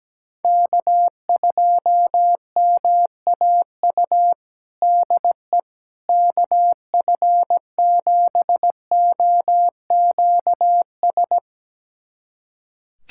Morse code, heard as a keyed tone, about 17 wpm, 700 Hz. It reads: K2MAU DE KF7OQS
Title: USA Callsign DE Callsign 17wpm